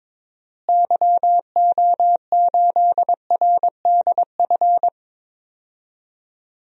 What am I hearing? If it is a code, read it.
Morse code, heard as a keyed tone, about 22 wpm, 700 Hz. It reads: YO8RDF
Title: DXCC-100 Call Signs - 6 Characters - Repeated 2x - 22wpm